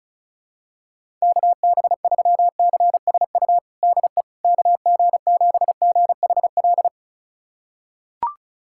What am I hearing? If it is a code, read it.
Morse code, heard as a keyed tone, about 36 wpm, 700 Hz. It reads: KB3CSU DE KG7GHL
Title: USA Callsign DE Callsign - 6 Characters - 35wpm